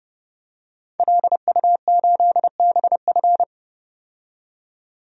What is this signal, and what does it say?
Morse code, heard as a keyed tone, about 32 wpm, 700 Hz. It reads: LU8BF
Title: DXCC-100 Call Signs - 5 Characters - 30wpm